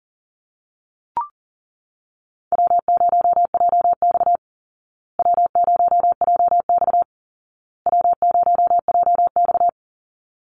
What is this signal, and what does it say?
Morse code, heard as a keyed tone, about 40 wpm, 700 Hz. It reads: W0JX W0JX W0JX
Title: Call Signs Repeated 2x - 40wpm